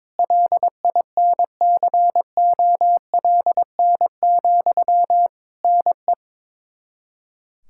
Morse code, heard as a keyed tone, about 22 wpm, 700 Hz. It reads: LINCOLN, NE
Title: Practice QSO - Exchange 1 - Beginning - 22wpm